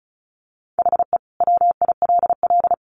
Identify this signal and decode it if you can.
Morse code, heard as a keyed tone, about 34 wpm, 700 Hz. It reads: HE WILL
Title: Sentences from Top 1100 Words 35wpm